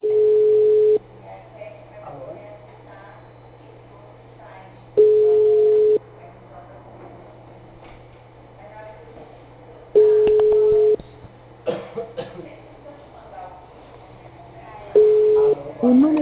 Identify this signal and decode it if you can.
Ringback tone (Europe, 425 Hz), with speech.